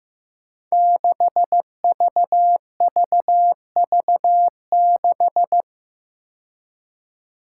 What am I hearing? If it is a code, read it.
Morse code, heard as a keyed tone, about 15 wpm, 700 Hz. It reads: 6VVV6
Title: Five Challenging Letter-Number Combinations 15wpm